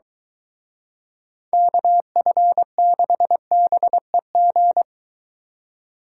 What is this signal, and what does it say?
Morse code, heard as a keyed tone, about 23 wpm, 700 Hz. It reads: KF6BEG